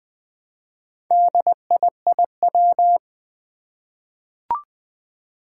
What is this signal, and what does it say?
Morse code, heard as a keyed tone, about 20 wpm, 700 Hz. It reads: DIIW
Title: Four Challenging Letter-Number Combinations 20wpm